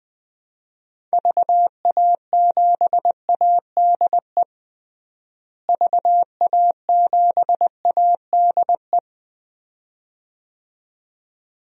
Morse code, heard as a keyed tone, about 20 wpm, 700 Hz. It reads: VA7ADE VA7ADE
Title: DXCC-100 Call Signs - 6 Characters - Repeated 2x - 20wpm